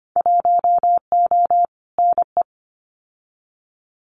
Morse code, heard as a keyed tone, about 25 wpm, 700 Hz. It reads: 1O NE
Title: Winter Field Day Exchanges 25wpm